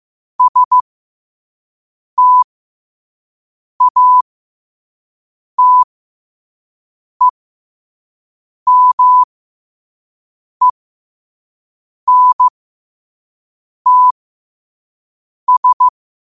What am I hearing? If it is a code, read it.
Morse code, heard as a keyed tone, about 15 wpm, spaced as for 5 wpm, 1000 Hz. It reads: STATEMENTS
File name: qotdfiles / 5wpm